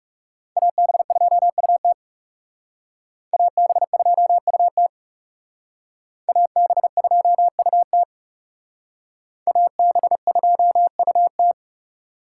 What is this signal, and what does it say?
Morse code, heard as a keyed tone, about 38 wpm, 700 Hz. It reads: AB2UT AB2UT AB2UT AB2UT
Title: Speed Racing - USA 2x2 Call Signs 30wpm